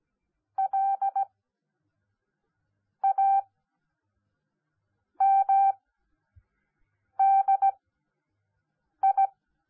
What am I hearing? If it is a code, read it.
Morse code, heard as a keyed tone, about 17 wpm, spaced as for 5 wpm, 780 Hz. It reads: LAMDI
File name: Onlinelehrgang / morsen / DJ4UF Morsen10